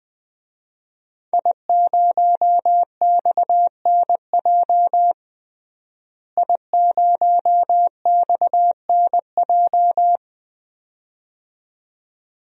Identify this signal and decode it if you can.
Morse code, heard as a keyed tone, about 21 wpm, 700 Hz. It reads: I0XNJ I0XNJ
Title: DXCC-100 Call Signs - 5 Characters - Repeated 2x - 20wpm